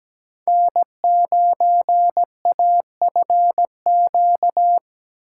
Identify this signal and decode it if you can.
Morse code, heard as a keyed tone, about 21 wpm, 700 Hz. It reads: N9AFQ
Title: Speed Racing - USA 1x3 Call Signs 17wpm